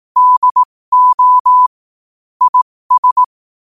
Morse code, heard as a keyed tone, about 18 wpm, spaced as for 15 wpm, 1000 Hz. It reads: DO IS
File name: qotdfiles / 15wpm